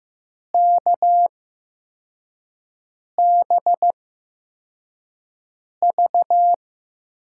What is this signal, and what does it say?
Morse code, heard as a keyed tone, about 15 wpm, spaced as for 4 wpm, 700 Hz. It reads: KBV
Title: USA Call Signs - 8x Character Spacing - 15wpm